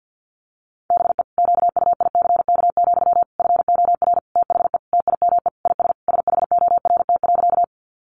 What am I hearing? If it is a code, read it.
Morse code, heard as a keyed tone, about 50 wpm, 700 Hz. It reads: BE QUICK, FOR THE TIME IS SHORT.
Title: Sentences from Top 500 Words 50wpm